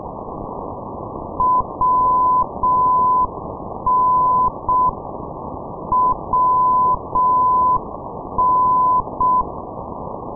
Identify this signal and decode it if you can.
Morse code, heard as a keyed tone, about 6 wpm, 1010 Hz, with noise. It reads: WN WN
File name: ndb minsk - machulishchi wn-458, sdr kiwi minsk